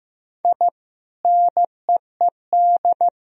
Morse code, heard as a keyed tone, about 15 wpm, 700 Hz. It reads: I NEED
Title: Sentences from Top 1000 Words - Review Part 1 - 15wpm